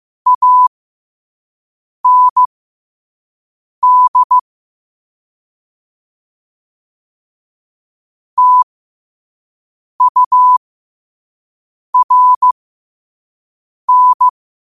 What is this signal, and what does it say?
Morse code, heard as a keyed tone, about 15 wpm, spaced as for 5 wpm, 1000 Hz. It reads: AND TURN
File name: qotdfiles / 5wpm